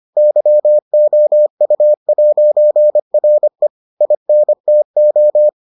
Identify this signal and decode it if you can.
Morse code, heard as a keyed tone, about 25 wpm, 600 Hz. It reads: YOU'RE INTO